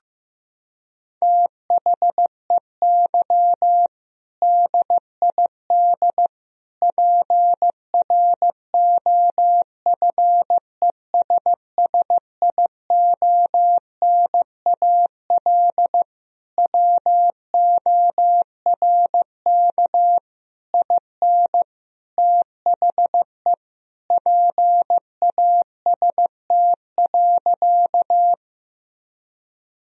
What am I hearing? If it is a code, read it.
Morse code, heard as a keyed tone, about 15 wpm, 700 Hz. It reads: THEY DID PROFESSIONAL WORK IN THE PAST.
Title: Sentences from Top 1000 Words - Review Part 4 - 15wpm